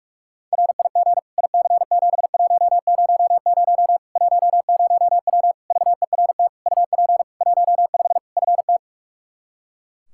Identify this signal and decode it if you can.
Morse code, heard as a keyed tone, about 46 wpm, 700 Hz. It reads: RIG IC7100 10W VERT UP 15 FT
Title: Practice QSO - Exchange 2 - Info - 45wpm